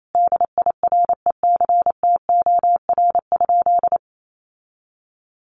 Morse code, heard as a keyed tone, about 28 wpm, 700 Hz. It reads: DIRECTOR?
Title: Sentences from Top 1300 Words 28wpm